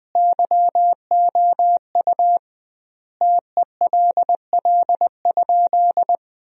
Morse code, heard as a keyed tone, about 20 wpm, 700 Hz. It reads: YOU TELL?
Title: 2x Word-Spacing - Sentences from Top 300 Words - 20wpm